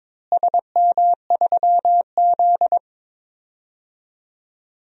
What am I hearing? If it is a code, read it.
Morse code, heard as a keyed tone, about 22 wpm, 700 Hz. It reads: SM3Z
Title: DXCC-100 Call Signs - 4 Characters - Repeated 2x - 22wpm